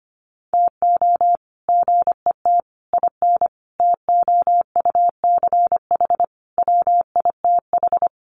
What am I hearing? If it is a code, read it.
Morse code, heard as a keyed tone, about 25 wpm, 700 Hz. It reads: TO GET IN TOUCH WITH
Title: Sets of 5 Words Encore2 25wpm